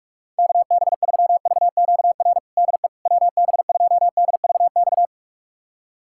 Morse code, heard as a keyed tone, about 46 wpm, 690 Hz. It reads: KB3VQR DE WB2DVX
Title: USA Callsign DE Callsign - 6 Characters - 45wpm